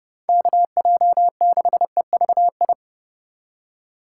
Morse code, heard as a keyed tone, about 30 wpm, 700 Hz. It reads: KJ6EVI
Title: USA Call Signs - 3x Character Spacing - 30wpm